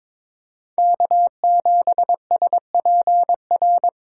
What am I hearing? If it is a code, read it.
Morse code, heard as a keyed tone, about 22 wpm, 700 Hz. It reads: K7SPR
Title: USA Call Signs - 4x Character Spacing - 22wpm